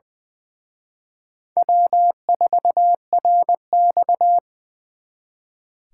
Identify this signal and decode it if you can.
Morse code, heard as a keyed tone, about 24 wpm, 700 Hz. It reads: W4RX
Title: Speed Racing - USA 1x2 Call Signs 20wpm